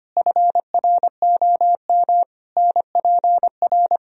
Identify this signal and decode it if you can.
Morse code, heard as a keyed tone, about 25 wpm, 700 Hz. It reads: FROM NPR
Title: Speed-Racing - Sets of 5 Words 25wpm